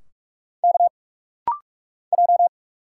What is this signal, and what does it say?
Morse code, heard as a keyed tone, about 44 wpm, 700 Hz. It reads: K J
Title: Single Letters - Rapid-Fire 45wpm